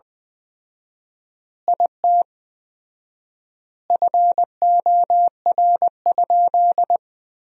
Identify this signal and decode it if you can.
Morse code, heard as a keyed tone, about 20 wpm, 700 Hz. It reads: IT FOR?
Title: 4x Word-Spacing - Sentences from Top 100 Words - 20wpm